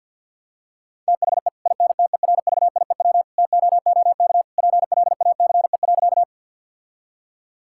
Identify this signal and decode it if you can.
Morse code, heard as a keyed tone, about 50 wpm, 700 Hz. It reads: THE INTERVIEW TOOK PLACE.
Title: Sentences from Top 1600 Words 50wpm